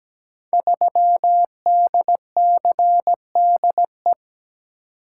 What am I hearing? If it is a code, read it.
Morse code, heard as a keyed tone, about 17 wpm, 700 Hz. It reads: 3DCDE
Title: Code Groups - 5 Letter:Numbers - 17wpm